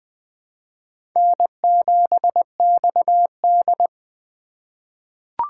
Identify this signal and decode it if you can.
Morse code, heard as a keyed tone, about 20 wpm, 700 Hz. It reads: N7XD
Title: USA Call Signs - 4 Characters - 4x Character Spacing - 20wpm